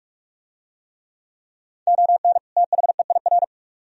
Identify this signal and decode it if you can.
Morse code, heard as a keyed tone, about 41 wpm, 700 Hz. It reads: ON THEIR